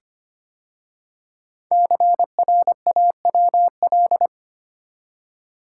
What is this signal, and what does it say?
Morse code, heard as a keyed tone, about 25 wpm, 700 Hz. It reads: CRAWL
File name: 5 Letter Words - First 3 Letters Given - 25wpm